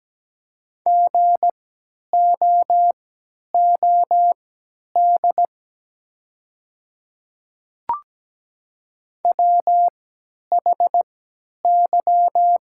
Morse code, heard as a keyed tone, about 17 wpm, spaced as for 10 wpm, 700 Hz. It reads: GOOD WHY
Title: 3x Character Spacing - Top 200 Words - 17wpm